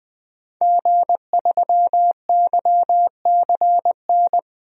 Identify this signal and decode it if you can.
Morse code, heard as a keyed tone, about 20 wpm, 700 Hz. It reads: G3YCN
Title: DXCC-100 Call Signs - 5 Characters - 20wpm